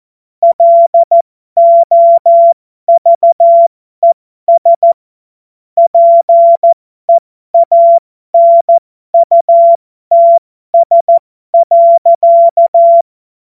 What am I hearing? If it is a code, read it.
Morse code, heard as a keyed tone, about 14 wpm, spaced as for 12 wpm, 680 Hz. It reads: LOVES PEANUTS.
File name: Morsekurs / Witze